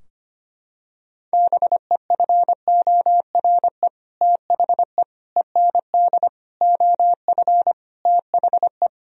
Morse code, heard as a keyed tone, about 25 wpm, 700 Hz. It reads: BEFORE THE END OF THE
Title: Sentences from Top 1000 Words - Review Part 2 - 25wpm